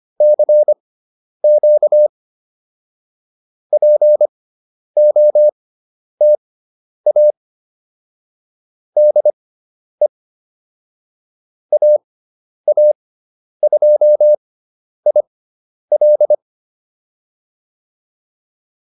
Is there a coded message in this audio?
Yes: Morse code from 0.2 to 16.3 s, heard as a keyed tone, about 25 wpm, spaced as for 10 wpm, 600 Hz. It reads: CQ POTA DE AA2IL